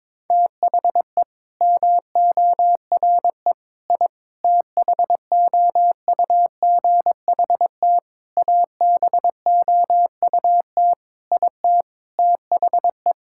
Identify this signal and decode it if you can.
Morse code, heard as a keyed tone, about 22 wpm, 700 Hz. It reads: THE MORE I THOUGHT ABOUT IT THE
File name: practice / 22wpm / Sets of 8 Words - 22wpm